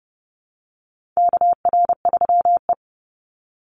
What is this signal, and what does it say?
Morse code, heard as a keyed tone, about 32 wpm, 700 Hz. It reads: KR3E
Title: Call Signs - 30wpm